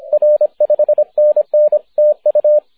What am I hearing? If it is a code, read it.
Morse code, heard as a keyed tone, about 28 wpm, 560 Hz, with radio noise. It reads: R 5NN TU
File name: wp / e4x 6m cw may 30th 2010